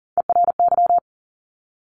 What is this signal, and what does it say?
Morse code, heard as a keyed tone, about 40 wpm, 700 Hz. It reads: ERY
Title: Top Three Letter Combinations in Words 40wpm